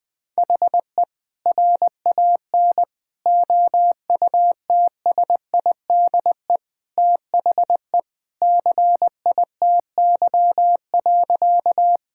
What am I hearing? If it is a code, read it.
Morse code, heard as a keyed tone, about 20 wpm, 700 Hz. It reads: HE RAN OUTSIDE THE CITY.
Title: Sentences from Top 1000 Words - Review Part 6 - 20wpm